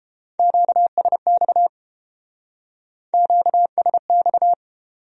Morse code, heard as a keyed tone, about 31 wpm, 700 Hz. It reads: QSX QSX